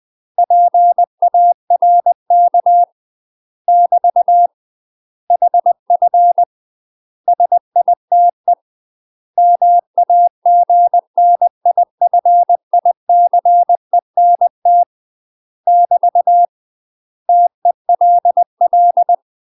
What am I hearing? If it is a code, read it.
Morse code, heard as a keyed tone, about 20 wpm, 700 Hz. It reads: PARK <BT> HF SITE MAGNIFICENT <BT> TELL
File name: UK MASTER QSOs TRAINING / 20WPM / UK MASTER QSOs 0035